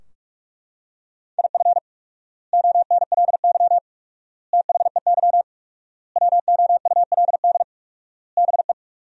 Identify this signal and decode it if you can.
Morse code, heard as a keyed tone, about 42 wpm, 700 Hz. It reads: IF ONLY THEY WOULD BE
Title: 4x Word-Spacing - Sentences from Top 500 Words - 45wpm